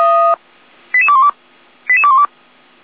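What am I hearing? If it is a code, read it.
Five-tone selective call (ZVEI), heard as quick runs of five tones, tones of about 70 ms. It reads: 88212 88212
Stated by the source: Five-tone ZVEI selective call 88237 received by SDR.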